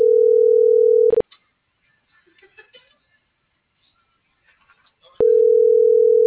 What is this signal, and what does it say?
Ringback tone (North America), 440 + 480 Hz.